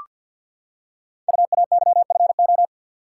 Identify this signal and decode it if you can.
Morse code, heard as a keyed tone, about 47 wpm, 700 Hz. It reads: UAYFO
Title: Code Groups - 5 Letters - 50wpm